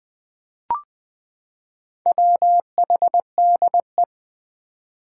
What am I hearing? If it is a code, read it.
Morse code, heard as a keyed tone, about 21 wpm, 700 Hz. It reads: WHDE
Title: Four Challenging Letter-Number Combinations 20wpm